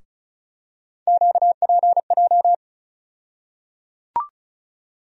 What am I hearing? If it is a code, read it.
Morse code, heard as a keyed tone, about 36 wpm, 700 Hz. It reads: QPJ
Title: USA Call Sign Postfixes - 3 Letters 35wpm